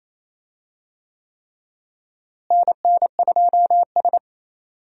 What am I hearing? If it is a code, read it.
Morse code, heard as a keyed tone, about 28 wpm, 700 Hz. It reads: NN2S
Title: USA Call Signs - 4 Characters - Repeated 3x - 28wpm